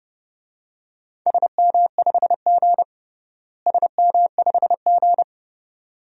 Morse code, heard as a keyed tone, about 30 wpm, 700 Hz. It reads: SM5G SM5G
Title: DXCC-100 Call Signs - 4 Characters - Repeated 2x - 30wpm